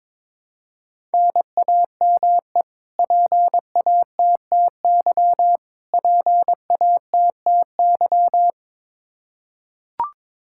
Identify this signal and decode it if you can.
Morse code, heard as a keyed tone, about 22 wpm, 700 Hz. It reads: NAME PATTY PATTY
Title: Practice QSO - Exchange 1 - Beginning - 22wpm